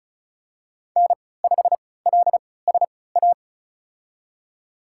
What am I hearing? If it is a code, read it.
Morse code, heard as a keyed tone, about 36 wpm, spaced as for 20 wpm, 700 Hz. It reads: N5LSA
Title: USA Call Signs - 5 Characters - 3x Character Spacing - 35wpm